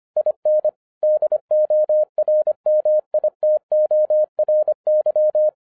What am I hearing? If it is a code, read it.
Morse code, heard as a keyed tone, about 25 wpm, 610 Hz. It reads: IN DORMITORY
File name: 2011-10-mp3 / n04-JUST MOVED 25wpm